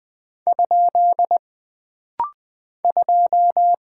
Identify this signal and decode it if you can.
Morse code, heard as a keyed tone, about 20 wpm, 700 Hz. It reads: ? 2
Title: Single Character (Letter:Number:Punctuation:Prosign) - Rapid-Fire - 20wpm